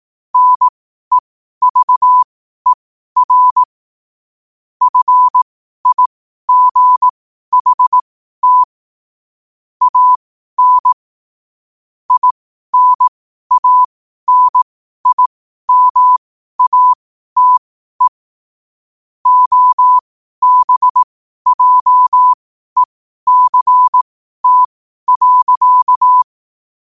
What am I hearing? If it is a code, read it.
Morse code, heard as a keyed tone, about 18 wpm, spaced as for 12 wpm, 1000 Hz. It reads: NEVER FIGHT AN INANIMATE OBJECT.
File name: qotdfiles / 12wpm